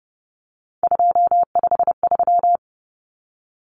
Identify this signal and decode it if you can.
Morse code, heard as a keyed tone, about 30 wpm, 700 Hz. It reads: 253
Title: Numbers - 3-Digit 30wpm